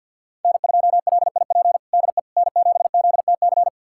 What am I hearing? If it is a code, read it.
Morse code, heard as a keyed tone, about 50 wpm, 700 Hz. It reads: N2LIP DE N7ZTC
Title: USA Callsign DE Callsign - 5 Characters - 50wpm